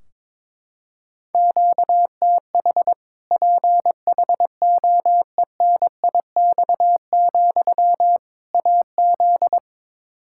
Morse code, heard as a keyed tone, about 22 wpm, 700 Hz. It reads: QTH PHOENIX, AZ
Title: Practice QSO - Exchange 1 - Beginning - 22wpm